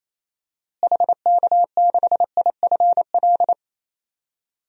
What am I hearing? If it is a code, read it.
Morse code, heard as a keyed tone, about 28 wpm, 700 Hz. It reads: HK6IFL
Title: DXCC-100 Call Signs - 6 Characters - Repeated 2x - 28wpm